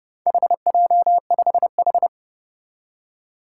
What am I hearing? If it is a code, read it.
Morse code, heard as a keyed tone, about 29 wpm, 700 Hz. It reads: HJ5H
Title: Four Challenging Letter-Number Combinations 30wpm